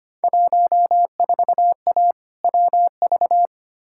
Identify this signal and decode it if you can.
Morse code, heard as a keyed tone, about 25 wpm, 700 Hz. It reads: 14A WV
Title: Field Day Exchanges Encore - 25wpm